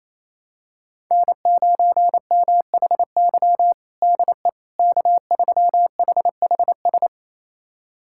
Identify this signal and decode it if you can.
Morse code, heard as a keyed tone, about 28 wpm, 700 Hz. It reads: N9MHY DE K3HHS
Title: USA Callsign DE Callsign - 5 Characters - 28wpm